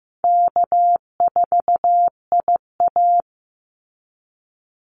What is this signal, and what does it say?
Morse code, heard as a keyed tone, about 15 wpm, 700 Hz. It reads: K4IA
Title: Easy Call Signs Repeated 3x - 15wpm